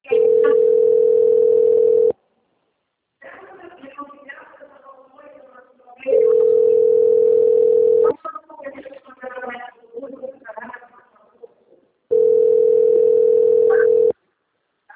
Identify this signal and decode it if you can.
Ringback tone (North America), 440 + 480 Hz, with speech.
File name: out- - - - - .14680